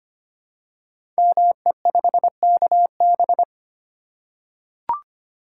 Morse code, heard as a keyed tone, about 25 wpm, 700 Hz. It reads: ME5KB
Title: Code Groups - 5 Letter:Numbers - 25wpm